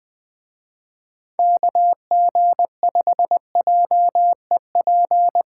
Morse code, heard as a keyed tone, about 20 wpm, 700 Hz. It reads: KG5JEP